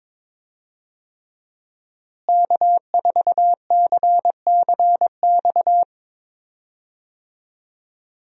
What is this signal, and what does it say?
Morse code, heard as a keyed tone, about 21 wpm, 690 Hz. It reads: K4CCX
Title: USA Call Signs - 5 Characters - Repeated 3x - 22wpm